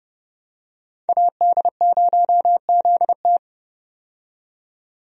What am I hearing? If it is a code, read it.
Morse code, heard as a keyed tone, about 31 wpm, 700 Hz. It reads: AD0ZT
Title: USA Call Signs - 5 Characters - 30wpm